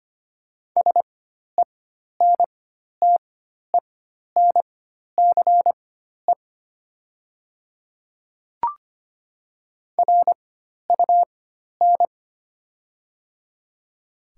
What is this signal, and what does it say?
Morse code, heard as a keyed tone, about 25 wpm, spaced as for 12 wpm, 700 Hz. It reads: SENTENCE RUN
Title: 4x Character Spacing - Top 300 Words - 25wpm